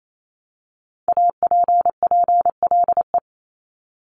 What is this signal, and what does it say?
Morse code, heard as a keyed tone, about 28 wpm, 700 Hz. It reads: APPLE
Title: Multi-letter Phonemes Repeated 3x 28wpm